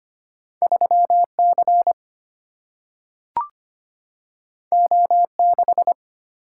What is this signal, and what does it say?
Morse code, heard as a keyed tone, about 25 wpm, 700 Hz. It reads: 3 C O 6
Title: Single-Letter Single-Number 25wpm